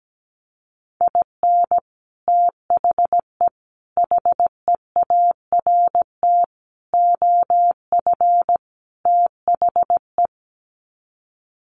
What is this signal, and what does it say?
Morse code, heard as a keyed tone, about 17 wpm, 700 Hz. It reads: IN THE HEART OF THE
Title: Sets of 5 Words 17wpm